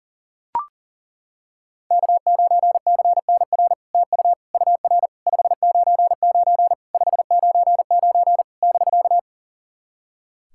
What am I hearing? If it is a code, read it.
Morse code, heard as a keyed tone, about 39 wpm, 700 Hz. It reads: K9CNR TU UR 599 599 <BK>
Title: SOTA QSO Practice 40wpm